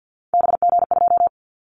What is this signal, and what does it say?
Morse code, heard as a keyed tone, about 50 wpm, 700 Hz. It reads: BG2
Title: Callsign Prefixes for Top 100 DXCC - 50wpm